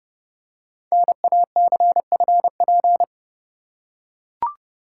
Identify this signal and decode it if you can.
Morse code, heard as a keyed tone, about 29 wpm, 700 Hz. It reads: NACFP
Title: Code Groups - 5 Letter:Numbers - 30wpm